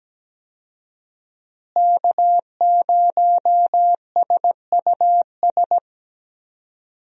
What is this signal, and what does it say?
Morse code, heard as a keyed tone, about 17 wpm, 700 Hz. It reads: K0SUS
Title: USA Call Signs - 5 Characters - Repeated 2x - 17wpm